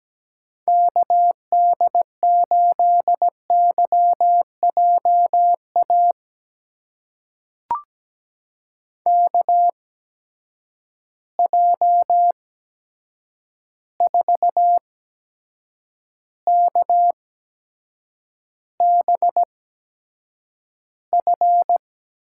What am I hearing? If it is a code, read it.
Morse code, heard as a keyed tone, about 17 wpm, spaced as for 7 wpm, 700 Hz. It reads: KD8YJA KJ4KBF
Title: USA Call Signs - 8x Character Spacing - 17wpm